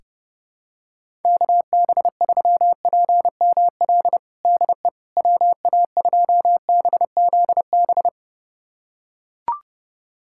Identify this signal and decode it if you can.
Morse code, heard as a keyed tone, about 30 wpm, 700 Hz. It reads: KB3PML DE WA2BZB
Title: USA Callsign DE Callsign - 6 Characters - 30wpm